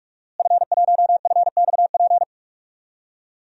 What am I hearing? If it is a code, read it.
Morse code, heard as a keyed tone, about 46 wpm, 700 Hz. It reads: F1FXP
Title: DXCC-100 Call Signs - 5 Characters - Repeated 3x - 45wpm